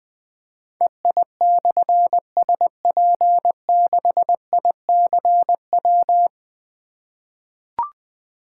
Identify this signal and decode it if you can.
Morse code, heard as a keyed tone, about 20 wpm, 700 Hz. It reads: EI/SP6ICW
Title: Difficult Call Signs 20wpm